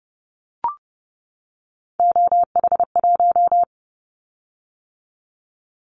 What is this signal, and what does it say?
Morse code, heard as a keyed tone, about 30 wpm, 700 Hz. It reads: OH1
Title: Callsign Prefixes for Top 50 DXCC - 30wpm